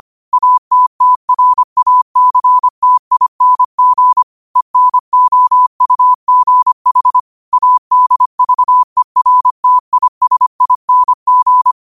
Morse code, heard as a keyed tone, about 25 wpm, 1000 Hz. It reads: ATTRACTING ENOUGH ADVERTISING
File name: qotdfiles / 25wpm